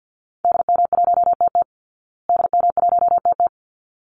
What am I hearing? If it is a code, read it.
Morse code, heard as a keyed tone, about 50 wpm, 700 Hz. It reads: DM1TT DM1TT
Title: Call Signs Repeated 3x - 50wpm